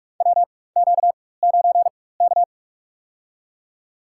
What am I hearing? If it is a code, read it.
Morse code, heard as a keyed tone, about 47 wpm, spaced as for 21 wpm, 690 Hz. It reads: WQ9K
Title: USA Call Signs - 4 Characters - 4x Character Spacing - 45wpm